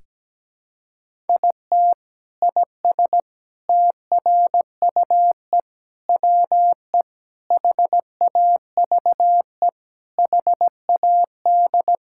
Morse code, heard as a keyed tone, about 17 wpm, 700 Hz. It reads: IT IS TRUE WE HAVE HAD
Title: Sentences from Top 1000 Words - Review Part 5 - 17wpm